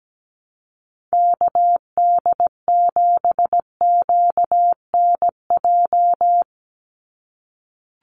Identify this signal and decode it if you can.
Morse code, heard as a keyed tone, about 17 wpm, 700 Hz. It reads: KD7QNJ
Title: USA Call Signs 17wpm